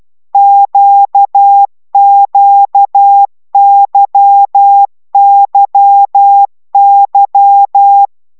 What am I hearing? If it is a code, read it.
Morse code, heard as a keyed tone, about 12 wpm, 800 Hz. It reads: QQYYY
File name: taviro tanulas / cw 75 15 10